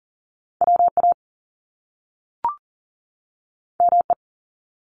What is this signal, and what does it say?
Morse code, heard as a keyed tone, about 41 wpm, 700 Hz. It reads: WA ME
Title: Top Two Letter Combinations in Words 40wpm